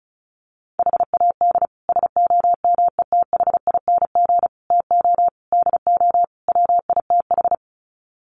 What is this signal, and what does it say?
Morse code, heard as a keyed tone, about 34 wpm, 700 Hz. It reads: HAD SOMETHING TO DO WITH
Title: Sets of 5 Words 35wpm